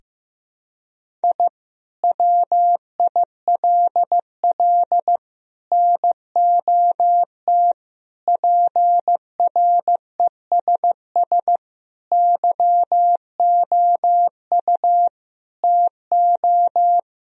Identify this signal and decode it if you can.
Morse code, heard as a keyed tone, about 15 wpm, 700 Hz. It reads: I WILL NOT PRESS YOU TO
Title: Sentences from Top 1000 Words - Review Part 1 - 15wpm